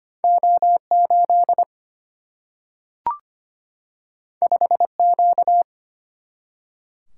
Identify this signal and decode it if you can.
Morse code, heard as a keyed tone, about 25 wpm, 700 Hz. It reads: O8 5Q
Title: Single-Letter Single-Number 25wpm